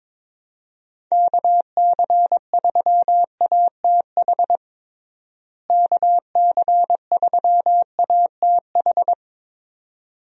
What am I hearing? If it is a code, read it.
Morse code, heard as a keyed tone, about 22 wpm, 700 Hz. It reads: KC3ATH KC3ATH
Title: USA Call Signs - 6 Characters - Repeated 2x - 22wpm